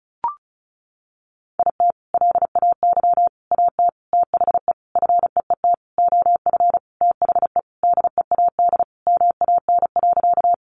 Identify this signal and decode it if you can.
Morse code, heard as a keyed tone, about 34 wpm, 700 Hz. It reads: IT LAY AT THE FEET OF THE DEAD MAN.
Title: Sentences from Top 1500 Words 35wpm